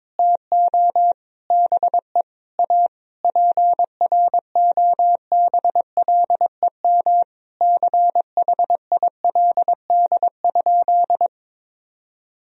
Morse code, heard as a keyed tone, about 22 wpm, 700 Hz. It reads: TO BE A PROBLEM CHILD?
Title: Sentences from Top 1000 Words - Review Part 2 - 22wpm